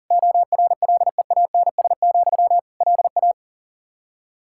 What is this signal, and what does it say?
Morse code, heard as a keyed tone, about 37 wpm, 700 Hz. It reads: ORLEANS, LA
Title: Practice QSO - Exchange 1 - Beginning - 40wpm